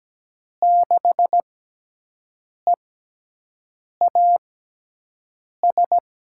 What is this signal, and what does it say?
Morse code, heard as a keyed tone, about 17 wpm, spaced as for 6 wpm, 700 Hz. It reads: 6EAS
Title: USA Call Signs - 6 Characters - 6x Character Spacing - 17wpm